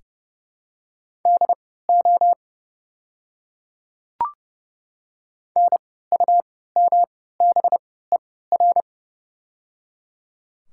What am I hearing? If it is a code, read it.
Morse code, heard as a keyed tone, about 30 wpm, spaced as for 17 wpm, 690 Hz. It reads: DO NUMBER
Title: 3x Character Spacing - Top 100 Words - 30wpm